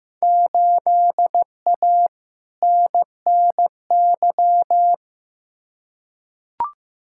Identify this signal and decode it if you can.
Morse code, heard as a keyed tone, about 15 wpm, 700 Hz. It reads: 8A NNY
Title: Field Day Exchanges Encore - 15wpm